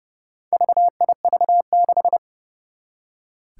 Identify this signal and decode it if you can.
Morse code, heard as a keyed tone, about 28 wpm, 700 Hz. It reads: VIV6
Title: Four Challenging Letter-Number Combinations 30wpm